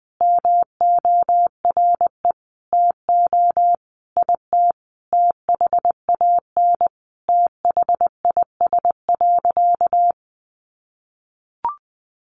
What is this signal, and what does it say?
Morse code, heard as a keyed tone, about 20 wpm, 700 Hz. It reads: MORE TO IT THAN THIS.
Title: Sentences from Top 100 Words 20wpm